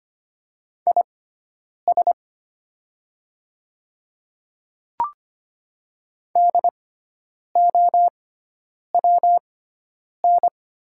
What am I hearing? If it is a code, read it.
Morse code, heard as a keyed tone, about 25 wpm, spaced as for 9 wpm, 700 Hz. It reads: IS DOWN
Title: 6x Character Spacing - Top 100 Words - 25wpm